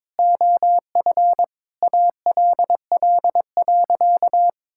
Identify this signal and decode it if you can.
Morse code, heard as a keyed tone, about 22 wpm, 700 Hz. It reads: OF ALL.
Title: Sentences from Top 1000 Words - Review Part 6 - 22wpm